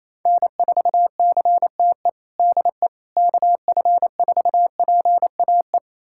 Morse code, heard as a keyed tone, about 28 wpm, 700 Hz. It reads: N4CTE DE KF4PAE
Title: USA Callsign DE Callsign 28wpm